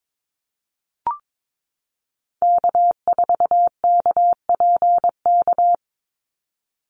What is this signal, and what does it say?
Morse code, heard as a keyed tone, about 22 wpm, 700 Hz. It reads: K4KPK
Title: NA SOTA Activator Callsigns 22wpm